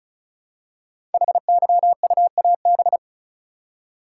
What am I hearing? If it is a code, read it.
Morse code, heard as a keyed tone, about 36 wpm, 700 Hz. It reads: HYUAB
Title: Code Groups - 5 Letter:Numbers - 35wpm